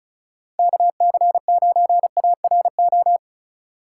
Morse code, heard as a keyed tone, about 34 wpm, 700 Hz. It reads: KC9ARO
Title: USA Call Signs - 6 Characters - 3x Character Spacing - 35wpm